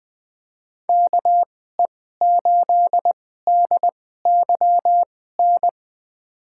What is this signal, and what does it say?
Morse code, heard as a keyed tone, about 20 wpm, spaced as for 15 wpm, 700 Hz. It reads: KE8DYN
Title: USA Call Signs - 6 Characters - 2x Character Spacing - 20wpm